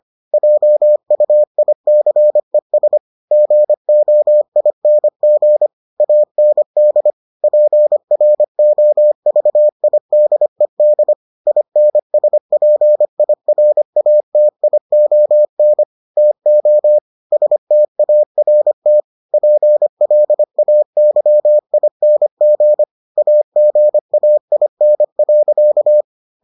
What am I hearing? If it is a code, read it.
Morse code, heard as a keyed tone, about 25 wpm, 600 Hz. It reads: JUICES GOING AND PROVIDED INSPIRATION TO START PLAYING AGAIN.